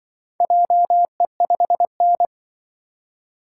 Morse code, heard as a keyed tone, about 25 wpm, 700 Hz. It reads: JE5N
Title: Speed-Racing - Four Letter-Number Combinations - 20wpm